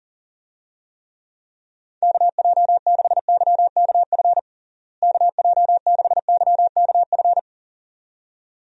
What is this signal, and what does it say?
Morse code, heard as a keyed tone, about 40 wpm, 700 Hz. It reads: KJ6YKF KJ6YKF
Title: USA Call Signs - 6 Characters - Repeated 2x - 40wpm